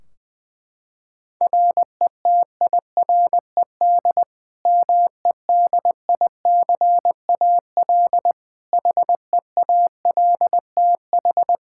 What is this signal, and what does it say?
Morse code, heard as a keyed tone, about 21 wpm, 700 Hz. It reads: RETIRED MEDICAL HEALTH
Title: Practice QSO - Exchange 3 - More Info - 20wpm